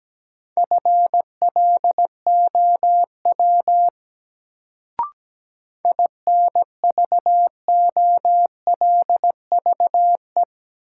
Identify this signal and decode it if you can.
Morse code, heard as a keyed tone, about 17 wpm, 700 Hz. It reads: FLOW INVOLVE
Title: Sentences from Top 1200 Words - New Word Drill - 17wpm